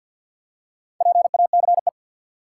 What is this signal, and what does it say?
Morse code, heard as a keyed tone, about 53 wpm, 700 Hz. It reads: PACE
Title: Four Letter Words - 50wpm